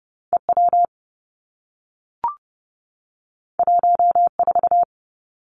Morse code, heard as a keyed tone, about 30 wpm, 700 Hz. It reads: E W 1 4
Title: Two Challenging Letter-Number Combinations 30wpm